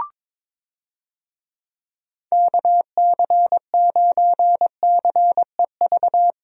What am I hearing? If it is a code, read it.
Morse code, heard as a keyed tone, about 22 wpm, 700 Hz. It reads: KC9CEV
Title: USA Call Signs - 6 Characters - Repeated 2x - 22wpm